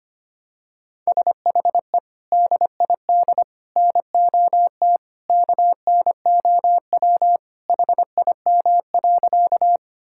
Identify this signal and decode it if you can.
Morse code, heard as a keyed tone, about 25 wpm, 700 Hz. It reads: SHE DID NOT KNOW HIM.
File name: practice / 25wpm / Sentences with 3-Word Phrases - 25wpm